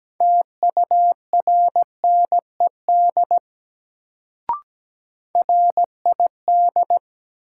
Morse code, heard as a keyed tone, about 17 wpm, 700 Hz. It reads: TURNED RID
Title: Sets of 4 Words - New Word Drill - 17wpm